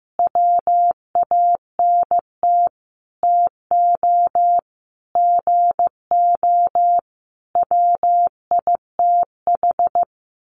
Morse code, heard as a keyed tone, about 15 wpm, 700 Hz. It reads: WANT TO GO WITH
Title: Sentences from Top 1300 Words 15wpm